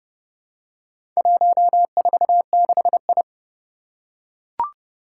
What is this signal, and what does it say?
Morse code, heard as a keyed tone, about 32 wpm, 700 Hz. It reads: 146I
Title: Four Challenging Letter-Number Combinations 30wpm